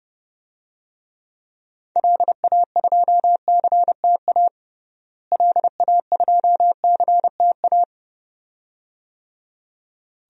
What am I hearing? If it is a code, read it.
Morse code, heard as a keyed tone, about 31 wpm, 700 Hz. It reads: LA2CTA LA2CTA
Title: DXCC-100 Call Signs - 6 Characters - Repeated 2x - 30wpm